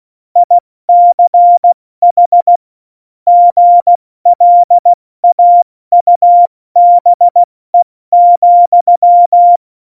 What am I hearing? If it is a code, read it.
Morse code, heard as a keyed tone, about 16 wpm, 700 Hz. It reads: ICH GLAUBE,
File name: Morsekurs / Witze / Witz 04. Juni 2019, 70 F80